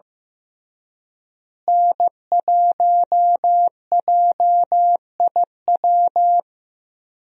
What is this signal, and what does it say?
Morse code, heard as a keyed tone, about 15 wpm, 700 Hz. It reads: N1JIW
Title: USA Call Signs - 5 Characters - Repeated 3x - 15wpm